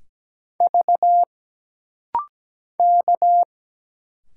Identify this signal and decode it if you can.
Morse code, heard as a keyed tone, about 17 wpm, 700 Hz. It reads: V K
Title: Single Letters - Rapid-Fire 17wpm